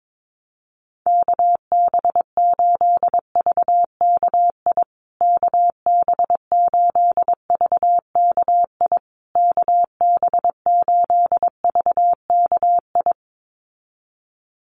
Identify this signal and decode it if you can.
Morse code, heard as a keyed tone, about 22 wpm, 700 Hz. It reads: KB8VKI KB8VKI KB8VKI
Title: USA Call Signs Repeated 3x 22wpm